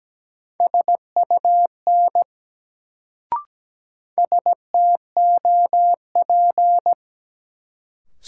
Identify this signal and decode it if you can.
Morse code, heard as a keyed tone, about 17 wpm, 700 Hz. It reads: SUN STOP
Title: Sentences from Top 300 Words - New Word Drill - 17wpm